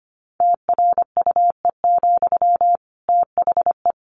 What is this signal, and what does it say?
Morse code, heard as a keyed tone, about 25 wpm, 700 Hz. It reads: TRUE, THE
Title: Sentences from Top 700 Words 25wpm